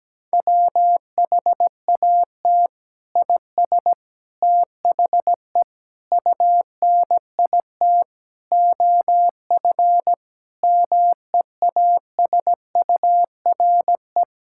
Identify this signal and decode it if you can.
Morse code, heard as a keyed tone, about 17 wpm, 700 Hz. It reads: WHAT IS THE UNIT OF MEASURE
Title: Sentences from Top 1000 Words - Review Part 6 - 17wpm